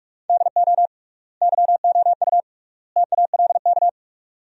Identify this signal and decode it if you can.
Morse code, heard as a keyed tone, about 46 wpm, 700 Hz. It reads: DO YOU TALK
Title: 3x Word-Spacing - Sentences from Top 500 Words - 45wpm